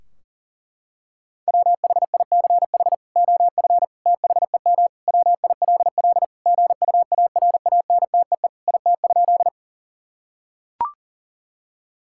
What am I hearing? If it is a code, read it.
Morse code, heard as a keyed tone, about 40 wpm, 700 Hz. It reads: WHICH OF THEM WILL GUARANTEE IT?
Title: Sentences from Top 1600 Words 40wpm